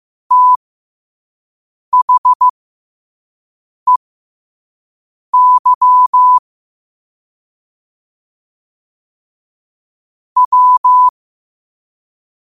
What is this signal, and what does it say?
Morse code, heard as a keyed tone, about 15 wpm, spaced as for 5 wpm, 1000 Hz. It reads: THEY W
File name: qotdfiles / 5wpm